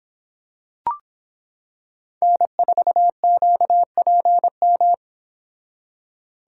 Morse code, heard as a keyed tone, about 26 wpm, 700 Hz. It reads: N4QPM